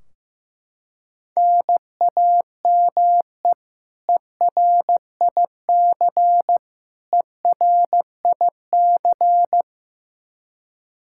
Morse code, heard as a keyed tone, about 15 wpm, 700 Hz. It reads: NAME ERIC ERIC
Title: Practice QSO - Exchange 1 - Beginning - 15wpm